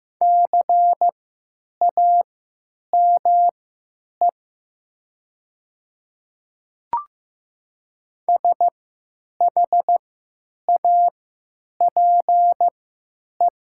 Morse code, heard as a keyed tone, about 15 wpm, spaced as for 9 wpm, 700 Hz. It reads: CAME SHAPE
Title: 3x Character Spacing - Top 500 Words - 15wpm